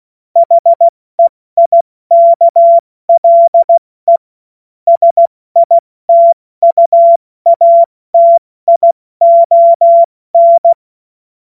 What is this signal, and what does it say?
Morse code, heard as a keyed tone, about 16 wpm, spaced as for 14 wpm, 680 Hz. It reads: HEIKLE SITUATION
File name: Morsekurs / Witze / Witz 30 Januar 2024 70 F80